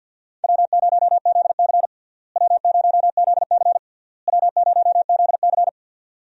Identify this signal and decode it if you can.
Morse code, heard as a keyed tone, about 48 wpm, 700 Hz. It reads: W0ZC W0ZC W0ZC
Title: USA Call Signs - 4 Characters - Repeated 3x - 50wpm